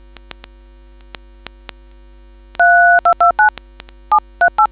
DTMF, heard as telephone touch tones, tones of about 100 ms. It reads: 3229 738